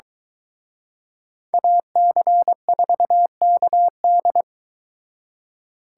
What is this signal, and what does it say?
Morse code, heard as a keyed tone, about 22 wpm, 700 Hz. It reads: AC4KD